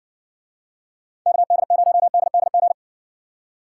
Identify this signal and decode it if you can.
Morse code, heard as a keyed tone, about 60 wpm, 700 Hz. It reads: KD0DDG